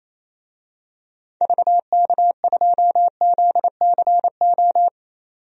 Morse code, heard as a keyed tone, about 28 wpm, 700 Hz. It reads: VK2ZCO
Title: DXCC-100 Call Signs - 6 Characters - 28wpm